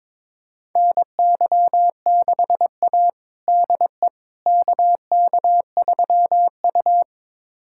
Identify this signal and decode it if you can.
Morse code, heard as a keyed tone, about 21 wpm, 700 Hz. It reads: NY6A DE KK3U
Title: USA Callsign DE Callsign - 4 Characters - 22wpm